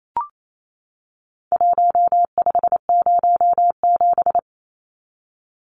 Morse code, heard as a keyed tone, about 28 wpm, 700 Hz. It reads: 1507
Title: Numbers - Contest Style - 4-Digit 28wpm